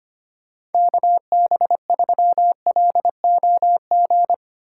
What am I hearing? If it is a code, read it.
Morse code, heard as a keyed tone, about 32 wpm, 700 Hz. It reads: KB3LOG